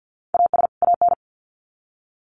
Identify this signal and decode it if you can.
Morse code, heard as a keyed tone, about 50 wpm, 700 Hz. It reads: AS AN
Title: Sets of 2 Words 50wpm